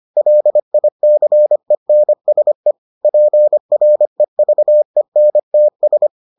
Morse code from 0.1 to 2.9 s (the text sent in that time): LICENSE